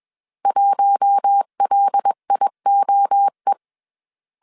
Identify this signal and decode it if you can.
Morse code, heard as a keyed tone, about 21 wpm, 800 Hz. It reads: 1LIOE
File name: CW / DET / 1 6 1500C S A 20wpm 2048